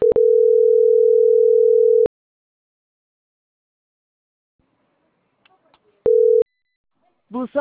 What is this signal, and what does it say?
Ringback tone (North America), 440 + 480 Hz.